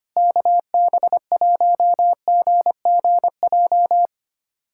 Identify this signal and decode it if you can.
Morse code, heard as a keyed tone, about 25 wpm, 700 Hz. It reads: KB1GGJ
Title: USA Callsign DE Callsign - 6 Characters - 25wpm